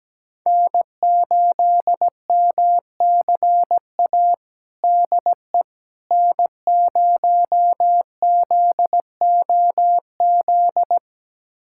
Morse code, heard as a keyed tone, about 17 wpm, 700 Hz. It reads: N8MCA DE N0ZOZ
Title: USA Callsign DE Callsign - 5 Characters - 17wpm